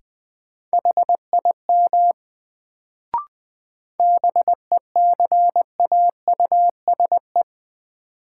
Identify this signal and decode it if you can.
Morse code, heard as a keyed tone, about 20 wpm, 700 Hz. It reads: HIM BECAUSE
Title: Sets of 2 Words - New Word Drill - 20wpm